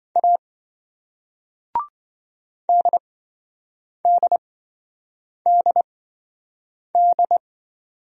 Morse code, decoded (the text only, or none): A DDDD